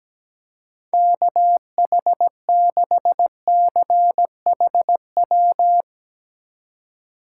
Morse code, heard as a keyed tone, about 17 wpm, 700 Hz. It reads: KH6CHW
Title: USA Call Signs - 6 Characters - 3x Character Spacing - 17wpm